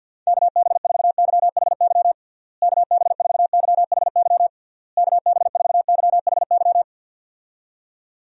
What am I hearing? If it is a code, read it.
Morse code, heard as a keyed tone, about 47 wpm, 700 Hz. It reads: KB4YHY KB4YHY KB4YHY
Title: USA Call Signs - 6 Characters - Repeated 3x - 50wpm